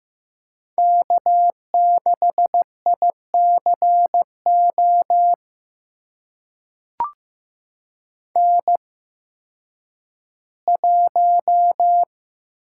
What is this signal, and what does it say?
Morse code, heard as a keyed tone, about 15 wpm, spaced as for 10 wpm, 700 Hz. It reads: K6ICO N1
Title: USA Call Signs - 5 Characters - 8x Character Spacing - 15wpm